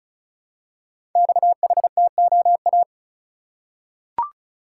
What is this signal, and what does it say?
Morse code, heard as a keyed tone, about 34 wpm, 700 Hz. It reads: XHTOA
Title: Code Groups - 5 Letters - 35wpm